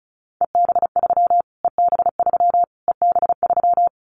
Morse code, heard as a keyed tone, about 34 wpm, 700 Hz. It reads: EB3 EB3 EB3
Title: Callsign Prefixes for Top 25 DXCC - Repeated 3x - 35wpm